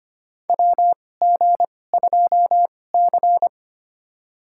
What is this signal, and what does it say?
Morse code, heard as a keyed tone, about 25 wpm, spaced as for 18 wpm, 700 Hz. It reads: WG2C
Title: USA Call Signs - 2x Character Spacing - 25wpm